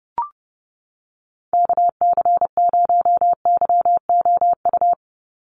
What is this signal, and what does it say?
Morse code, heard as a keyed tone, about 30 wpm, 700 Hz. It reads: KC0YOU
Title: USA Call Signs 30wpm